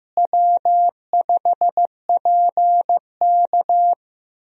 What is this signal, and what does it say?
Morse code, heard as a keyed tone, about 15 wpm, 700 Hz. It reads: W5PK
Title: USA Call Signs - 4 Characters - Repeated 3x - 15wpm